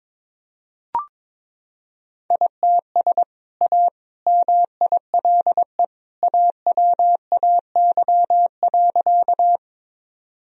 Morse code, heard as a keyed tone, about 22 wpm, 700 Hz. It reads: ITS A MILE AWAY.
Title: Sentences from Top 1000 Words - Review Part 1 - 22wpm